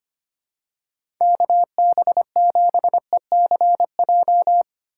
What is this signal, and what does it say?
Morse code, heard as a keyed tone, about 25 wpm, 700 Hz. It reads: KB7ECJ
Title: USA Call Signs - 6 Characters - 4x Character Spacing - 25wpm